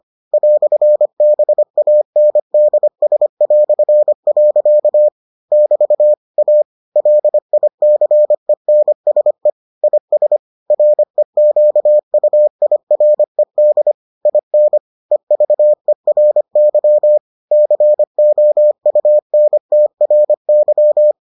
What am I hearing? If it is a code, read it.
Morse code, heard as a keyed tone, about 25 wpm, 600 Hz. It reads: "BANDS". <BT> A LICENSE IS REQUIRED IN EVERY COUNTRY